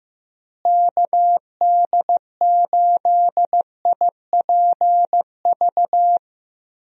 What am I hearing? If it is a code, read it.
Morse code, heard as a keyed tone, about 15 wpm, 700 Hz. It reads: KD8IPV
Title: USA Call Signs - 6 Characters - Repeated 2x - 15wpm